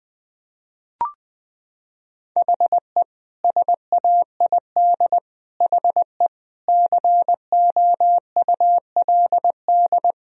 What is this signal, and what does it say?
Morse code, heard as a keyed tone, about 20 wpm, 700 Hz. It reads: HE SAID HE COULD
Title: Sentences from Top 1000 Words - Review Part 2 - 20wpm